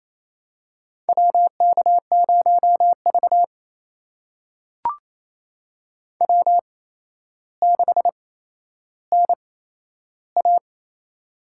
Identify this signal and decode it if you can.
Morse code, heard as a keyed tone, about 28 wpm, spaced as for 12 wpm, 700 Hz. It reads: WK0V W6NA
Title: USA Call Signs - 4 Characters - 8x Character Spacing - 28wpm